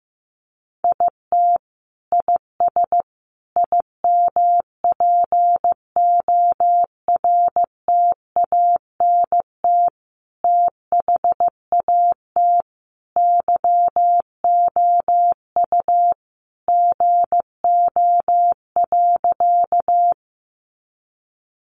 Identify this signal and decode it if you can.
Morse code, heard as a keyed tone, about 15 wpm, 700 Hz. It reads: IT IS IMPORTANT THAT YOU GO.
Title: Sentences from Top 600 Words 15wpm